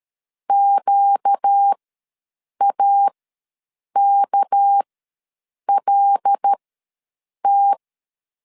Morse code, heard as a keyed tone, about 13 wpm, spaced as for 7 wpm, 800 Hz. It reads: QAKLT
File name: LEZIONI / 7 GF9Z / 7 GF9Z ws 12wpm 3321